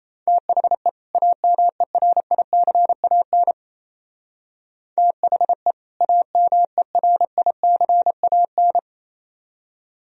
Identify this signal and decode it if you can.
Morse code, heard as a keyed tone, about 37 wpm, 700 Hz. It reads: THE AMERICAN THE AMERICAN